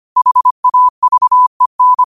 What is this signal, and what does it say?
Morse code, heard as a keyed tone, about 25 wpm, 1000 Hz. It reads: HAVEN
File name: qotdfiles / 25wpm